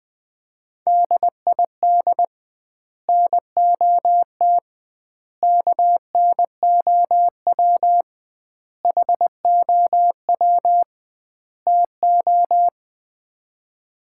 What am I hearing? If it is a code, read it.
Morse code, heard as a keyed tone, about 20 wpm, 700 Hz. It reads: DID NOT KNOW HOW TO
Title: 2x Word-Spacing - Sets of 5 Words - 20wpm